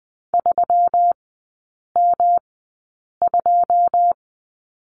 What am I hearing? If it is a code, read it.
Morse code, heard as a keyed tone, about 20 wpm, spaced as for 8 wpm, 690 Hz. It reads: 3M2
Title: 145 Lesson 24e - Letter y - Three Characters 20wpm